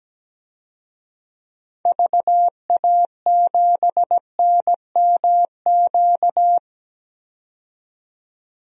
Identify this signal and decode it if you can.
Morse code, heard as a keyed tone, about 17 wpm, 700 Hz. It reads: VA7NMQ
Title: DXCC-100 Call Signs - 6 Characters - 17wpm